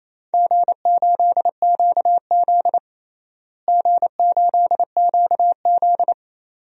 Morse code, heard as a keyed tone, about 28 wpm, 700 Hz. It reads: G8QZ G8QZ
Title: DXCC-100 Call Signs - 4 Characters - Repeated 2x - 28wpm